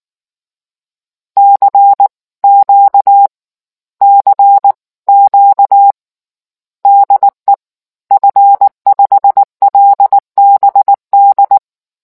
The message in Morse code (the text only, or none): CQ CQ DE F5LBD